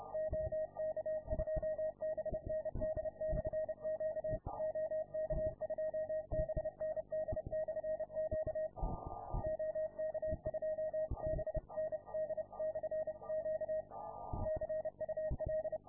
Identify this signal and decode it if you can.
Morse code, heard as a keyed tone, about 30 wpm, 630 Hz. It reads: OK2BND/Q OK2BND/Q OK2BND/Q CUL